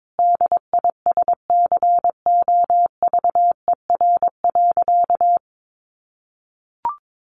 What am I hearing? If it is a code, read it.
Morse code, heard as a keyed tone, about 22 wpm, 700 Hz. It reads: DISCOVER.
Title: Sentences from Top 1100 Words 22wpm